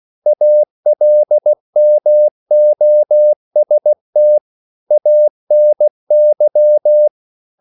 Morse code, heard as a keyed tone, about 16 wpm, 600 Hz. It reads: ALMOST ANY
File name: CW / Practice / EnglishQuotations0216